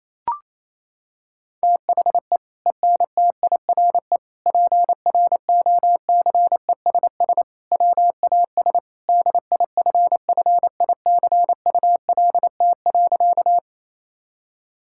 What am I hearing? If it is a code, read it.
Morse code, heard as a keyed tone, about 28 wpm, 700 Hz. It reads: THE ENTIRE PROCESS WAS DIFFICULT.
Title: Sentences from Top 1000 Words - Review Part 5 - 28wpm